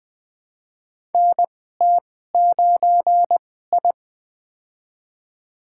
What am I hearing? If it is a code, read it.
Morse code, heard as a keyed tone, about 20 wpm, spaced as for 14 wpm, 700 Hz. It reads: NT9I
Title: USA Call Signs - 2x Character Spacing - 20wpm